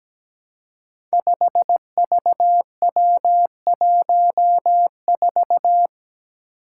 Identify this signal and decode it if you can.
Morse code, heard as a keyed tone, about 17 wpm, 700 Hz. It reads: 5VW14
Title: Five Challenging Letter-Number Combinations 17wpm